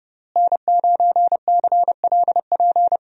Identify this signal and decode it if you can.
Morse code, heard as a keyed tone, about 30 wpm, 700 Hz. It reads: N9CLP
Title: USA Call Signs - 5 Characters - Repeated 2x - 30wpm